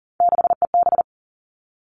Morse code, heard as a keyed tone, about 40 wpm, 700 Hz. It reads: 6EB
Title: Three Challenging Letter-Number Combinations 40wpm